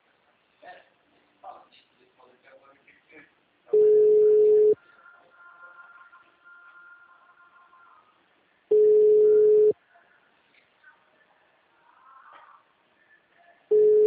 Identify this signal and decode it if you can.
Ringback tone (Europe, 425 Hz), with speech.